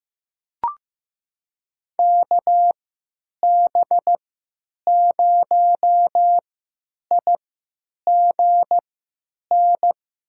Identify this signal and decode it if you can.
Morse code, heard as a keyed tone, about 15 wpm, spaced as for 9 wpm, 700 Hz. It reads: KB0IGN
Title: USA Call Signs - 3x Character Spacing - 15wpm